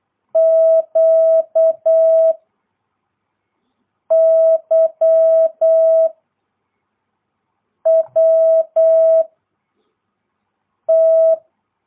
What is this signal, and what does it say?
Morse code, heard as a keyed tone, about 8 wpm, 650 Hz. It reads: Q Y W T
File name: jh8cbh / 2022cw